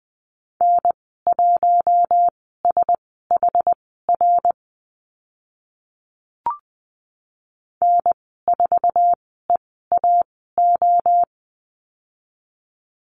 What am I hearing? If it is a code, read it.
Morse code, heard as a keyed tone, about 20 wpm, spaced as for 15 wpm, 700 Hz. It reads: N1SHR N4EAO
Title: 062 Lesson 11g - Letter h - Call Signs 20wpm